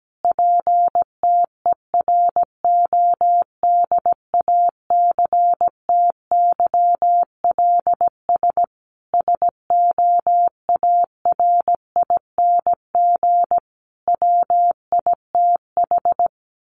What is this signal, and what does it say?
Morse code, heard as a keyed tone, about 17 wpm, 700 Hz. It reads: PTERODACTYLS SOARING WITH
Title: A Sound Of Thunder 17wpm